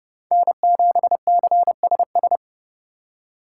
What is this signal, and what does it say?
Morse code, heard as a keyed tone, about 30 wpm, 700 Hz. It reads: N7CSS